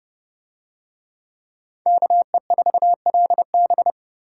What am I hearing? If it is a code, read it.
Morse code, heard as a keyed tone, about 27 wpm, 700 Hz. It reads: KE4LB